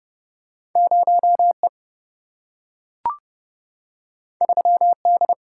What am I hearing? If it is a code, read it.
Morse code, heard as a keyed tone, about 30 wpm, 700 Hz. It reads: 0E 3D
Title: Single-Letter Single-Number 30wpm